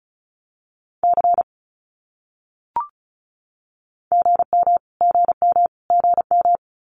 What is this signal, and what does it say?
Morse code, heard as a keyed tone, about 34 wpm, 700 Hz. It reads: C GM GM GM
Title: Abbreviations Repeated 3x - 35wpm